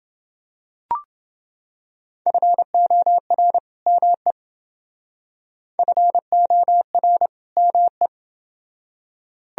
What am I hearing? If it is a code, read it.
Morse code, heard as a keyed tone, about 29 wpm, 700 Hz. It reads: FOR ME FOR ME